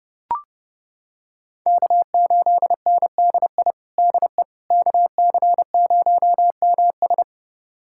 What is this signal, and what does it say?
Morse code, heard as a keyed tone, about 30 wpm, 700 Hz. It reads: K8NDI DE KC0MS
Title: USA Callsign DE Callsign - 5 Characters - 30wpm